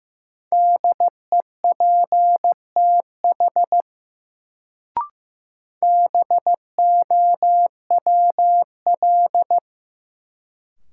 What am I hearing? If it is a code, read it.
Morse code, heard as a keyed tone, about 15 wpm, 700 Hz. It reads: DEPTH BOWL
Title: Sentences from Top 1200 Words - New Word Drill - 15wpm